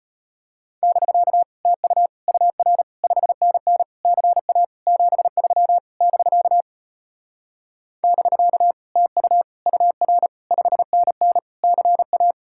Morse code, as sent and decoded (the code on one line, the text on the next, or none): -...-.- / - ..- / ..- .-. / ..... -. -. / -.-. .- / --... ...-- / -...-.- / -...-.- / - ..- / ..- .-. / ..... -. -. / -.-. .-
<BK> TU UR 5NN CA 73 <BK> <BK> TU UR 5NN CA